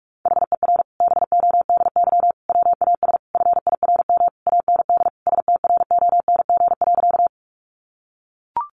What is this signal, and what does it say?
Morse code, heard as a keyed tone, about 44 wpm, 700 Hz. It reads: HER BODY WAS FIRM AND STRONG.
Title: Sentences from Top 1400 Words 45wpm